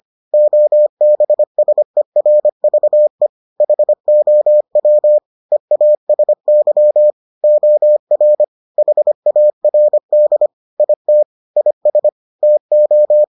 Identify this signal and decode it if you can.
Morse code, heard as a keyed tone, about 25 wpm, 600 Hz. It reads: OBSERVE HOW EASY OR HARD IT IS TO